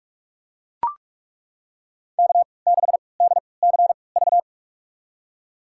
Morse code, heard as a keyed tone, about 43 wpm, spaced as for 25 wpm, 700 Hz. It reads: K6DCV
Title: USA Call Signs - 5 Characters - 3x Character Spacing - 45wpm